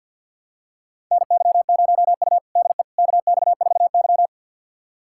Morse code, heard as a keyed tone, about 52 wpm, 700 Hz. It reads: NY0U DE KX4Y
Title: USA Callsign DE Callsign - 4 Characters - 50wpm